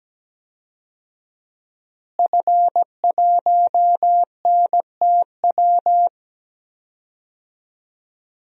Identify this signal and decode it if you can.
Morse code, heard as a keyed tone, about 17 wpm, 700 Hz. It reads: F1NTW
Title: DXCC-100 Call Signs - 5 Characters - Repeated 2x - 17wpm